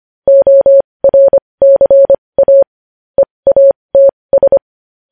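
Morse code, heard as a keyed tone, about 25 wpm, spaced as for 20 wpm, 550 Hz. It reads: ORCA EATS
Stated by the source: News headlines sent in Morse code at 20.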